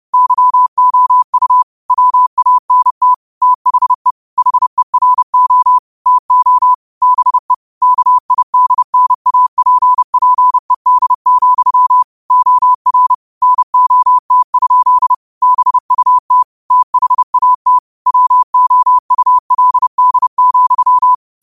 Morse code, heard as a keyed tone, about 30 wpm, 1000 Hz. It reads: YOU WANT THE HERO TO BE KIDNAPPED, OR NOT? BUT THAT WOULD,
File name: qotdfiles / 30wpm